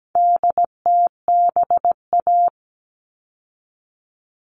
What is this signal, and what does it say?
Morse code, heard as a keyed tone, about 17 wpm, 700 Hz. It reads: DTBA
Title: All Cut Numbers - 4-Digit 17wpm